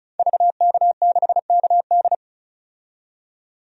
Morse code, heard as a keyed tone, about 36 wpm, 700 Hz. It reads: VK6KD
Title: DXCC-100 Call Signs - 5 Characters - 35wpm